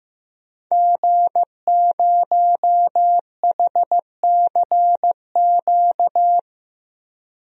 Morse code, heard as a keyed tone, about 16 wpm, 700 Hz. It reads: G0HCQ
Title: DXCC-100 Call Signs - 5 Characters - Repeated 3x - 15wpm